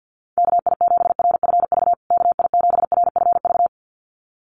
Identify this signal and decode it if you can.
Morse code, heard as a keyed tone, about 50 wpm, 700 Hz. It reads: KI7RFV KI7RFV
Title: USA Call Signs Repeated 2x 50wpm